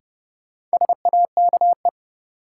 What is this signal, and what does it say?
Morse code, heard as a keyed tone, about 31 wpm, 700 Hz. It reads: SAKE
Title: Sets of 4 Words - New Word Drill - 30wpm